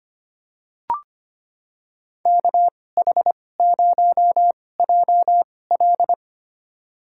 Morse code, heard as a keyed tone, about 25 wpm, spaced as for 18 wpm, 700 Hz. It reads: KH0JL
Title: USA Call Signs - 5 Characters - 2x Character Spacing - 25wpm